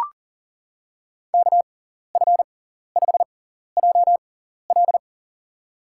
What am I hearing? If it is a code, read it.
Morse code, heard as a keyed tone, about 40 wpm, spaced as for 14 wpm, 700 Hz. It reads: KF5JL
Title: USA Call Signs - 5 Characters - 6x Character Spacing - 40wpm